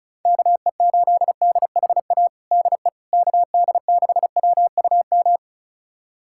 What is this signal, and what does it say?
Morse code, heard as a keyed tone, about 36 wpm, 690 Hz. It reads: KE8DHA DE KD6WUM
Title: USA Callsign DE Callsign - 6 Characters - 35wpm